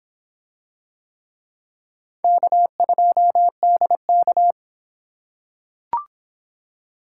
Morse code, heard as a keyed tone, about 26 wpm, 700 Hz. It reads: K2DK